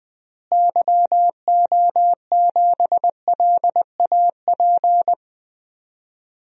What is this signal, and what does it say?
Morse code, heard as a keyed tone, about 20 wpm, 700 Hz. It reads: YO7LAP
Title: DXCC-100 Call Signs - 6 Characters - Repeated 3x - 20wpm